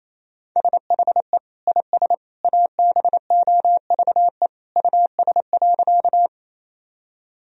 Morse code, heard as a keyed tone, about 28 wpm, 700 Hz. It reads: SHE IS ABOVE US.
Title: Sentences from Top 1000 Words - Review Part 2 - 28wpm